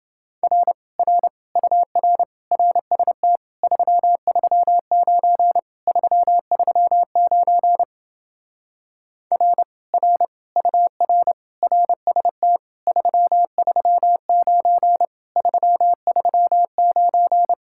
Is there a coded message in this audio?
Yes: Morse code from 0.4 to 17.5 s, heard as a keyed tone, about 28 wpm, 700 Hz. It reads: R R UR RST 339 339 R R UR RST 339 339